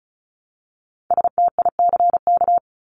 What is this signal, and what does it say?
Morse code, heard as a keyed tone, about 34 wpm, 700 Hz. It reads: STICK
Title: Top 1001-1250 Words 35wpm